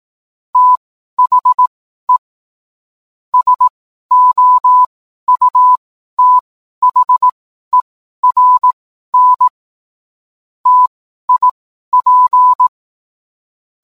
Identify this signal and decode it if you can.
Morse code, heard as a keyed tone, about 18 wpm, spaced as for 12 wpm, 1000 Hz. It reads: THE SOUTHERN TIP
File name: cwblog / cw mp3 directory / mem morse 12